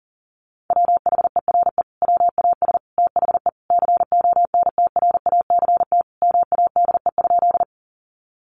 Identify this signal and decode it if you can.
Morse code, heard as a keyed tone, about 37 wpm, 700 Hz. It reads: WHERE WAS THE CONTRACT MADE?
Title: Sentences from Top 1400 Words 40wpm